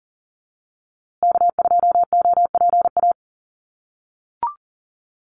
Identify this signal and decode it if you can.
Morse code, heard as a keyed tone, about 38 wpm, 700 Hz. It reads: K2OPA
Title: USA Call Signs Repeated 3x 40wpm